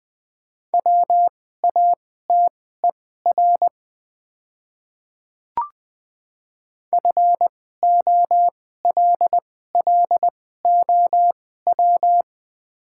Morse code, heard as a keyed tone, about 21 wpm, spaced as for 15 wpm, 700 Hz. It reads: WATER FOLLOW
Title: 2x Character Spacing - Top 200 Words - 20wpm